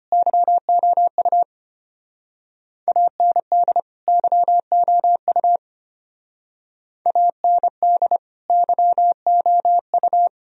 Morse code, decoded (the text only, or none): YOU AND YOU AND YOU